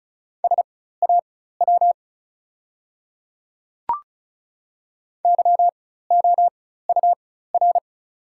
Morse code, heard as a keyed tone, about 36 wpm, spaced as for 16 wpm, 700 Hz. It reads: SAW YOUR
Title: 4x Character Spacing - Top 300 Words - 35wpm